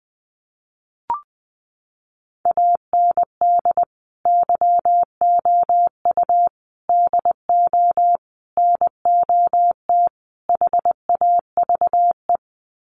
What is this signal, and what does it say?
Morse code, heard as a keyed tone, about 20 wpm, 700 Hz. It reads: AND YOU DO NOT HAVE
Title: Sets of 5 Words Encore 20wpm